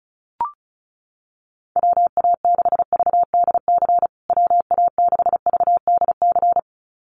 Morse code, heard as a keyed tone, about 34 wpm, 700 Hz. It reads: WA6VDC WA6VDC
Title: USA Call Signs Repeated 2x 35wpm